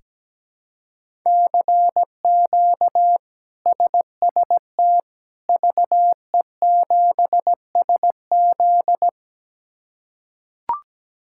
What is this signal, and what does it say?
Morse code, heard as a keyed tone, about 17 wpm, 700 Hz. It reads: CQ SST VE7SZ
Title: K1USN SST QSO Practice - 17wpm